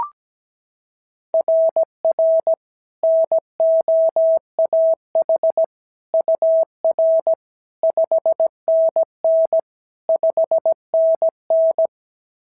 Morse code, heard as a keyed tone, about 16 wpm, 650 Hz. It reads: RR NOAH UR 5NN 5NN
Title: SOTA QSO Practice 17wpm